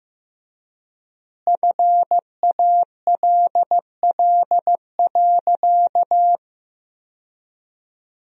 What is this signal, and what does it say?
Morse code, heard as a keyed tone, about 15 wpm, 700 Hz. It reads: FALL.
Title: 3x Word-Spacing - Sentences from Top 500 Words - 15wpm